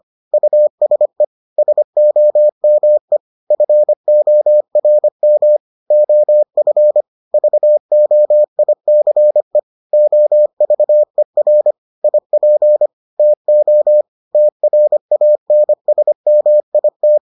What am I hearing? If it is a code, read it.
Morse code, heard as a keyed tone, about 25 wpm, 600 Hz. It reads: USE SOME FORM OF VOICE OVER IP TO TRANSMIT